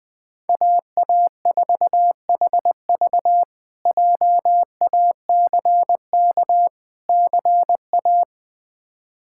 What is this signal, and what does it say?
Morse code, heard as a keyed tone, about 20 wpm, 700 Hz. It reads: AA4HV JACK CA
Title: K1USN SST QSO Practice - 20wpm